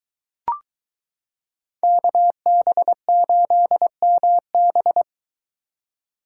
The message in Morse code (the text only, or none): KB8MB